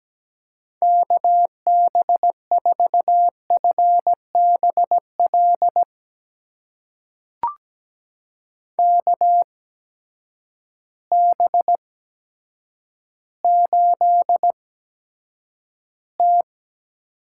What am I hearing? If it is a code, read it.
Morse code, heard as a keyed tone, about 17 wpm, spaced as for 8 wpm, 700 Hz. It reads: KB4FBL KB8T
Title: USA Call Signs - 6 Characters - 8x Character Spacing - 17wpm